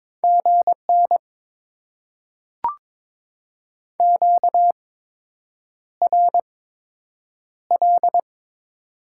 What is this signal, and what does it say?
Morse code, heard as a keyed tone, about 22 wpm, spaced as for 8 wpm, 690 Hz. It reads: GN QRL
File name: Top 90 QSO Elements - 8x Character Spacing - 22wpm